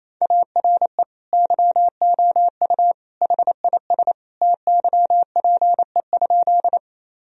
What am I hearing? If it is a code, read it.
Morse code, heard as a keyed tone, about 28 wpm, 700 Hz. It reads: ARE YOU HIS TYPE?
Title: Sentences from Top 1000 Words - Review Part 3 - 28wpm